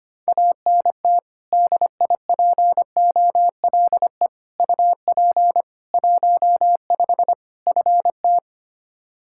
Morse code, heard as a keyed tone, about 25 wpm, 700 Hz. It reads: ANT DIPOLE UP 15 FT
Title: Practice QSO - Exchange 2 - Info - 25wpm